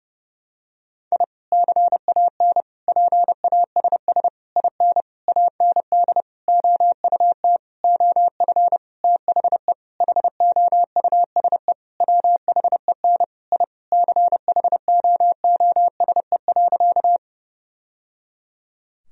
Morse code, heard as a keyed tone, about 32 wpm, 700 Hz. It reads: I CAN PASS IN AND OUT OF THE HOUSE WHEN I CHOOSE.
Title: Sentences from Top 1000 Words - Review Part 4 - 30wpm